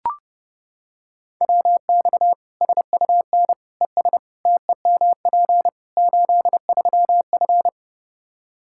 Morse code, heard as a keyed tone, about 32 wpm, 700 Hz. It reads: WX SUN ES TEMP 83F
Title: K1USN SST QSO Practice 30wpm